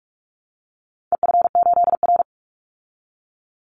Morse code, heard as a keyed tone, about 44 wpm, 700 Hz. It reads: EF8R
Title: Call Signs Repeated 2x - 45wpm